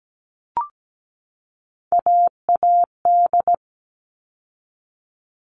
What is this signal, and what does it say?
Morse code, heard as a keyed tone, about 17 wpm, 700 Hz. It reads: AAD
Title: All Cut Numbers - 3-Digit 17wpm